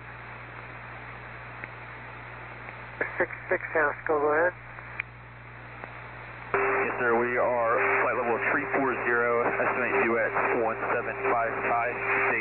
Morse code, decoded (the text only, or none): none